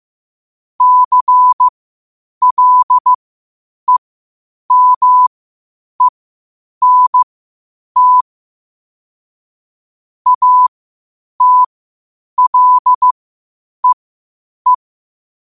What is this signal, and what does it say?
Morse code, heard as a keyed tone, about 15 wpm, spaced as for 8 wpm, 1000 Hz. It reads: CLEMENT ATLEE
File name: qotdfiles / 8wpm